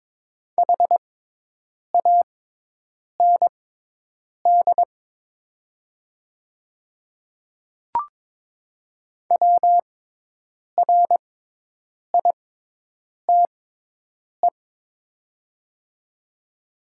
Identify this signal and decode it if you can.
Morse code, heard as a keyed tone, about 22 wpm, spaced as for 8 wpm, 700 Hz. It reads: HAND WRITE
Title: 6x Character Spacing - Top 300 Words - 22wpm